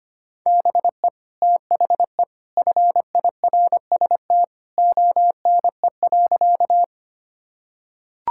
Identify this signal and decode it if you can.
Morse code, heard as a keyed tone, about 25 wpm, 700 Hz. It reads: BE THE FIRST ONE.
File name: practice / 25wpm / Sentences with 2-Word Phrases - 25wpm